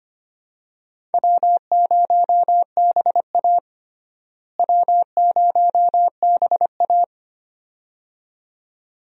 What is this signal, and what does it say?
Morse code, heard as a keyed tone, about 25 wpm, 700 Hz. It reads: W0BA W0BA
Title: USA Call Signs - 4 Characters - Repeated 2x - 25wpm